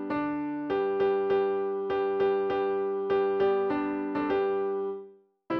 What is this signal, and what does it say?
There is no code.